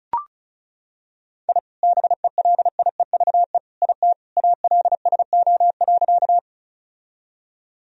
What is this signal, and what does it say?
Morse code, heard as a keyed tone, about 36 wpm, 700 Hz. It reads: I BELIEVE IT ALSO.
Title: Sentences from Top 1000 Words - Review Part 1 - 35wpm